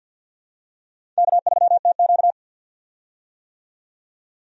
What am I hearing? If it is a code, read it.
Morse code, heard as a keyed tone, about 52 wpm, 700 Hz. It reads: K3TQ
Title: USA Call Signs - 4 Characters - 50wpm